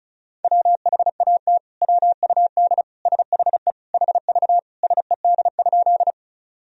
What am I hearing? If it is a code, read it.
Morse code, heard as a keyed tone, about 36 wpm, 700 Hz. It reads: WHAT WUD SHE HV SED?
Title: Sentences from Top 100 Words and Abbreviations 35wpm